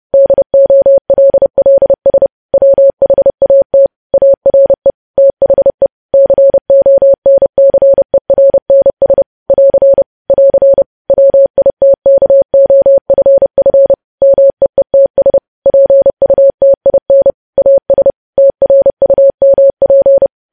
Morse code, heard as a keyed tone, about 30 wpm, 550 Hz. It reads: DOLLS WHAT ARE THE CONCERNS <AR> <AR> WITKOFF MEETS PUTIN AS TRUMP
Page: news headlines in morse code at 30 wpm